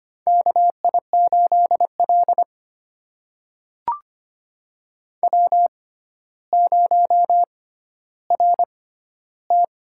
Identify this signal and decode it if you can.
Morse code, heard as a keyed tone, about 25 wpm, spaced as for 13 wpm, 700 Hz. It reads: KI8L W0RT
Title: USA Call Signs - 4 Characters - 6x Character Spacing - 25wpm